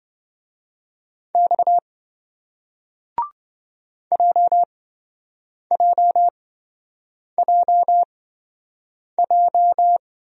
Morse code, heard as a keyed tone, about 27 wpm, spaced as for 7 wpm, 700 Hz. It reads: X JJJJ